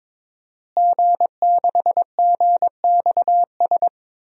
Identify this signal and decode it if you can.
Morse code, heard as a keyed tone, about 22 wpm, 700 Hz. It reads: G6GXS
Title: DXCC-100 Call Signs - 5 Characters - Repeated 2x - 22wpm